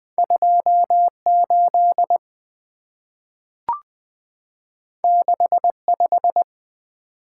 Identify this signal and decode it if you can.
Morse code, heard as a keyed tone, about 20 wpm, 700 Hz. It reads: 28 65
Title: Two Letter-Number Combinations - 20wpm